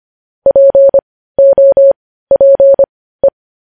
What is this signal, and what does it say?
Morse code, heard as a keyed tone, about 25 wpm, spaced as for 15 wpm, 550 Hz. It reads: POPE
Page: news headlines in morse code at 15 wpm